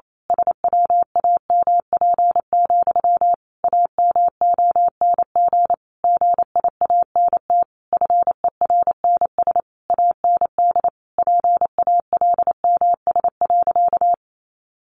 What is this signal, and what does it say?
Morse code, heard as a keyed tone, about 28 wpm, 700 Hz. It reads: SWAMP, AMONG GIANT FERNS AND PALMS.
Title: A Sound Of Thunder 28wpm